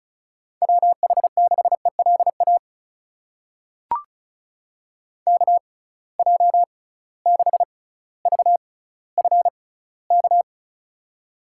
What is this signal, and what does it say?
Morse code, heard as a keyed tone, about 34 wpm, spaced as for 18 wpm, 700 Hz. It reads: WH6ELA KJ6VFK
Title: USA Call Signs - 6 Characters - 6x Character Spacing - 35wpm